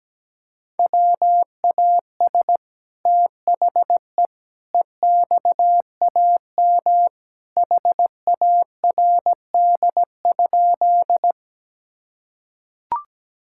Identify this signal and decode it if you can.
Morse code, heard as a keyed tone, about 17 wpm, 700 Hz. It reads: WAS THE EXAM HARD?
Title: Sentences from Top 1000 Words - Review Part 5 - 17wpm